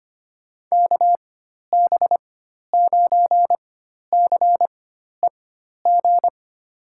Morse code, heard as a keyed tone, about 25 wpm, spaced as for 12 wpm, 700 Hz. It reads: KB9CEG
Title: USA Call Signs - 4x Character Spacing - 25wpm